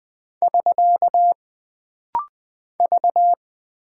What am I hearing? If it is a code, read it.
Morse code, heard as a keyed tone, about 20 wpm, 700 Hz. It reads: <SK> V
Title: Single Character (Letter:Number:Punctuation:Prosign) - Rapid-Fire - 20wpm